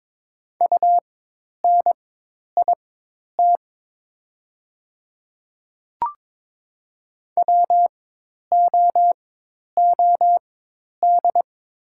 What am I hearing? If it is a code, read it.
Morse code, heard as a keyed tone, about 22 wpm, spaced as for 10 wpm, 700 Hz. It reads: UNIT WOOD
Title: 4x Character Spacing - Top 500 Words - 22wpm